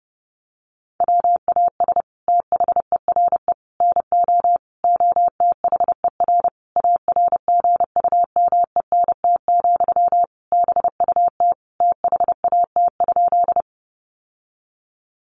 Morse code, heard as a keyed tone, about 30 wpm, 700 Hz. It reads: WAS THERE NO OTHER ARGUMENT, BUT THAT?
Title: Sentences from Top 1400 Words 30wpm